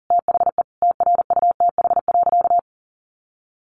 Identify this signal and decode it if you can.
Morse code, heard as a keyed tone, about 40 wpm, 700 Hz. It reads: THE TRUTH.
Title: Sentences from Top 1400 Words 40wpm